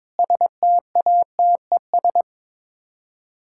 Morse code, heard as a keyed tone, about 22 wpm, 700 Hz. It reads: STATES
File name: practice / 22wpm / Sets of 7 Words - 22wpm